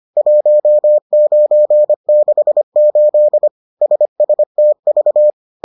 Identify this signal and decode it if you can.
Morse code, heard as a keyed tone, about 25 wpm, 600 Hz. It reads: 1968 SSTV